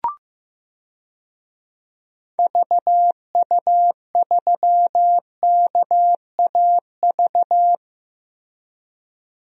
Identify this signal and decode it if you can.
Morse code, heard as a keyed tone, about 15 wpm, 700 Hz. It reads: VU3KAV
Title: DXCC-100 Call Signs - 6 Characters - Repeated 3x - 15wpm